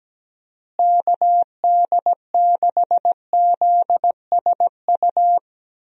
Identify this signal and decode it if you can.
Morse code, heard as a keyed tone, about 17 wpm, 700 Hz. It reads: KD6ZSU
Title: USA Call Signs - 6 Characters - Repeated 2x - 17wpm